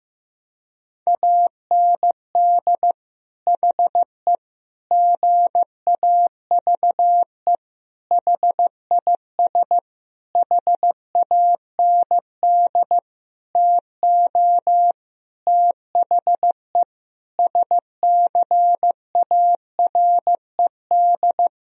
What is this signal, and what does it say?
Morse code, heard as a keyed tone, about 15 wpm, 700 Hz. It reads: AND HE GAVE HIS HAND TO THE SCARED
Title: Sentences from Top 1000 Words - Review Part 4 - 15wpm